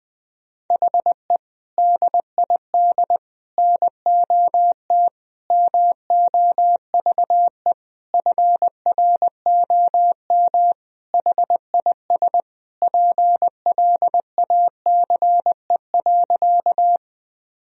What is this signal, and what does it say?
Morse code, heard as a keyed tone, about 21 wpm, 700 Hz. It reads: HE DID NOT MOVE FROM HIS PLACE.
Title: Sentences from Top 1000 Words - Review Part 1 - 20wpm